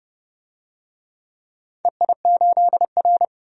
Morse code, heard as a keyed tone, about 32 wpm, 700 Hz. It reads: EI8R
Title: DXCC-100 Call Signs - 4 Characters - 30wpm